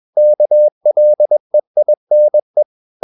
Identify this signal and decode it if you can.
Morse code, heard as a keyed tone, about 21 wpm, 600 Hz. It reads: KLEINE
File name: marienkind2 21wpm